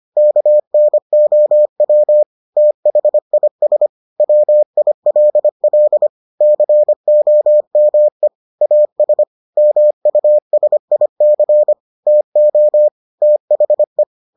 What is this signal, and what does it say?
Morse code, heard as a keyed tone, about 25 wpm, 600 Hz. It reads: KNOW THIS WILL COME AS MUSIC TO THE